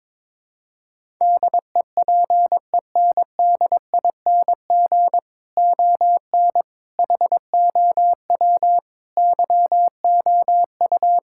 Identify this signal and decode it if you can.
Morse code, heard as a keyed tone, about 22 wpm, 700 Hz. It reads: DEPENDING ON HOW YOU
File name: practice / 22wpm / Sets of 7 Words - 22wpm